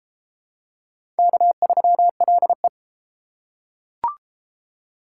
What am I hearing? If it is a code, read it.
Morse code, heard as a keyed tone, about 30 wpm, 700 Hz. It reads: K3LE